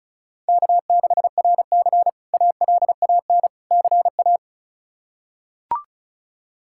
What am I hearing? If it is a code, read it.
Morse code, heard as a keyed tone, about 34 wpm, 700 Hz. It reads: K6RC ALAN CA
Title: K1USN SST QSO Practice - 35wpm